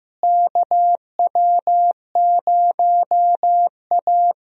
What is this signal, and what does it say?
Morse code, heard as a keyed tone, about 15 wpm, 700 Hz. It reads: KW0A
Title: USA Call Signs - 4 Characters - Repeated 3x - 15wpm